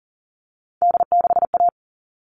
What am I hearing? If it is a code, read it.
Morse code, heard as a keyed tone, about 40 wpm, 700 Hz. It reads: D6A
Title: All Cut Numbers - 3-Digit 40wpm